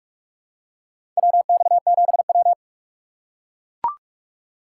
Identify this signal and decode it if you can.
Morse code, heard as a keyed tone, about 41 wpm, 690 Hz. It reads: WX7W
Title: USA Call Signs - 4 Characters - 3x Character Spacing - 45wpm